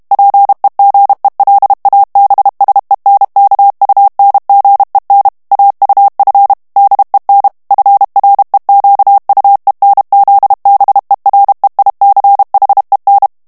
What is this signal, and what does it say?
Morse code, heard as a keyed tone, about 32 wpm, 800 Hz. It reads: PEGELABSENKUNGEN AUF DEN FREQUENZBEREICHEN
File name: ALLEMAND texte clair 1 vitesse 30 mots minute